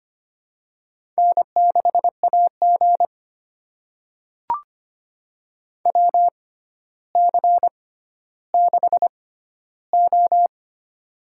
Morse code, heard as a keyed tone, about 25 wpm, spaced as for 13 wpm, 700 Hz. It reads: N6AG WC6O
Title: USA Call Signs - 4 Characters - 6x Character Spacing - 25wpm